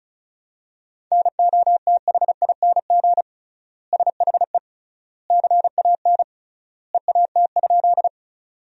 Morse code, heard as a keyed tone, about 34 wpm, 700 Hz. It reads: NOTHING SHE CAN EAT?
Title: 3x Word-Spacing - Sentences from Top 500 Words - 35wpm